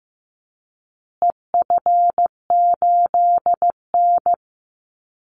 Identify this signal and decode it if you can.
Morse code, heard as a keyed tone, about 15 wpm, 700 Hz. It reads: EF8N
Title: Call Signs Repeated 3x - 15wpm